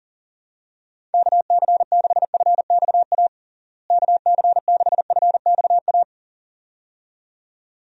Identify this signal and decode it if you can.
Morse code, heard as a keyed tone, about 40 wpm, 700 Hz. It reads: KC6FXA KC6FXA
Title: USA Call Signs - 6 Characters - Repeated 2x - 40wpm